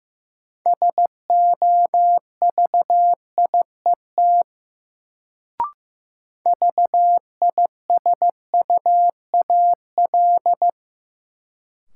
Morse code, heard as a keyed tone, about 16 wpm, 700 Hz. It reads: SOVIET VISUAL
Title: Sets of 5 Words Encore - New Word Drill - 15wpm